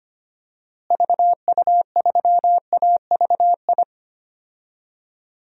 Morse code, heard as a keyed tone, about 25 wpm, 700 Hz. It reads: VU3AVI
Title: DXCC-100 Call Signs - 6 Characters - Repeated 2x - 25wpm